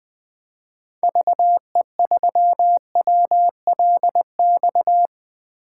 Morse code, heard as a keyed tone, about 20 wpm, 700 Hz. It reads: VE3WLX
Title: 2023 POTA Activator Call Signs - 20wpm